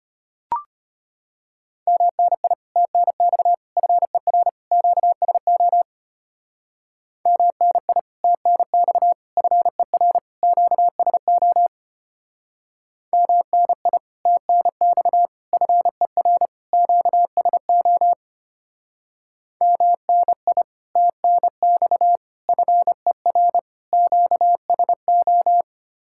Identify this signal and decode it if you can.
Morse code, heard as a keyed tone, about 32 wpm, 700 Hz. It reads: MNI TNX FER QSO MNI TNX FER QSO MNI TNX FER QSO MNI TNX FER QSO